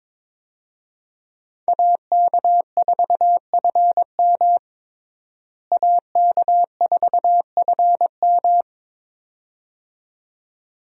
Morse code, heard as a keyed tone, about 22 wpm, 700 Hz. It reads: AK4FM AK4FM
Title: USA Call Signs - 5 Characters - Repeated 2x - 22wpm